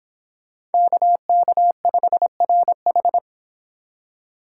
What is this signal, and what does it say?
Morse code, heard as a keyed tone, about 31 wpm, 700 Hz. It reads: KK5RH